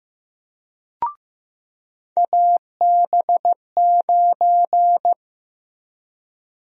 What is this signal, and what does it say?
Morse code, heard as a keyed tone, about 15 wpm, 700 Hz. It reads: AB9
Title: USA Call Sign Prefixes - Repeated 2x - 15wpm